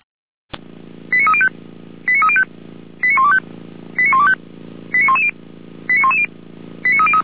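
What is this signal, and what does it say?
Five-tone selective call (ZVEI), heard as quick runs of five tones, tones of about 70 ms. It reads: 89305 89305 89125 89125 89110 89110 89303
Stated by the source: ZVEI five-tone call received on SDR.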